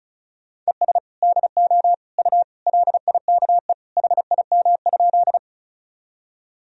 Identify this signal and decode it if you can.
Morse code, heard as a keyed tone, about 36 wpm, 700 Hz. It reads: ES DO U LIKE HIM?
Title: Sentences from Top 100 Words and Abbreviations 35wpm